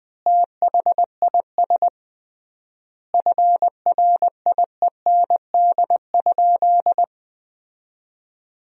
Morse code, heard as a keyed tone, about 20 wpm, 700 Hz. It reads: THIS FRIEND?
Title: 3x Word-Spacing - Sentences from Top 300 Words - 20wpm